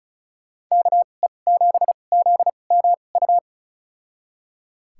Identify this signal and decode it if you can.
Morse code, heard as a keyed tone, about 36 wpm, spaced as for 25 wpm, 700 Hz. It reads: KE7ZMU
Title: USA Call Signs - 6 Characters - 2x Character Spacing - 35wpm